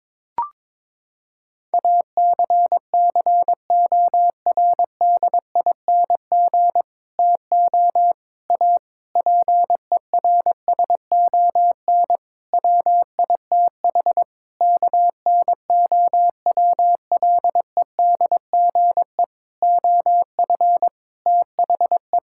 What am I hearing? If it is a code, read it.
Morse code, heard as a keyed tone, about 22 wpm, 700 Hz. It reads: ACCORDING TO A PERSON WITH KNOWLEDGE OF THE
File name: Sets of 9 Words - 22wpm